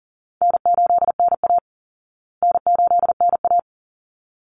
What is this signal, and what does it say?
Morse code, heard as a keyed tone, about 40 wpm, 700 Hz. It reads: N8NA N8NA
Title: Easy Call Signs Repeated 2x - 40wpm